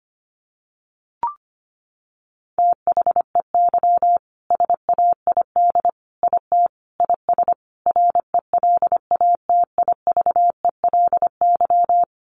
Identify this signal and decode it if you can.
Morse code, heard as a keyed tone, about 25 wpm, 700 Hz. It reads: THEY SAID IT IS RELATIVELY
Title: Sentences from Top 1100 Words 25wpm